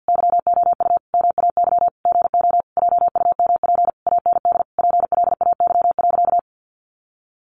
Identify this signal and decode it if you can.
Morse code, heard as a keyed tone, about 50 wpm, 700 Hz. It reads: YOU MAY GO JUMP AND PLAY.
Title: Sentences from Top 1300 Words 50wpm